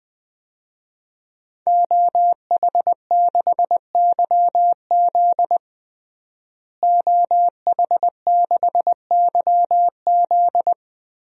Morse code, heard as a keyed tone, about 20 wpm, 700 Hz. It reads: OH6YZ OH6YZ
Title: DXCC-100 Call Signs - 5 Characters - Repeated 2x - 20wpm